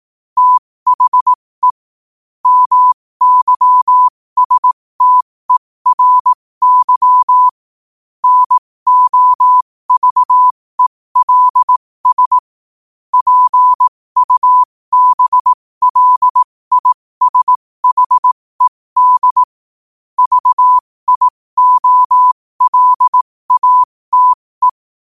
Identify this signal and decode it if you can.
Morse code, heard as a keyed tone, about 18 wpm, spaced as for 15 wpm, 1000 Hz. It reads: THE MYSTERY NOVELS PUBLISHED VIOLATE
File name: qotdfiles / 15wpm